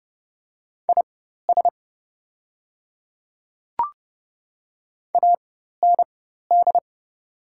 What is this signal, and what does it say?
Morse code, heard as a keyed tone, about 29 wpm, spaced as for 14 wpm, 700 Hz. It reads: IS AND
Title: 4x Character Spacing - Top 100 Words - 30wpm